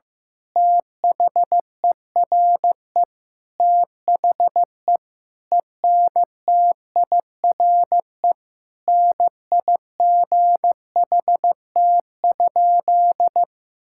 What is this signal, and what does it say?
Morse code, heard as a keyed tone, about 15 wpm, 700 Hz. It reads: THERE THE ENTIRE NIGHT?
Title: Sentences from Top 1000 Words - Review Part 5 - 15wpm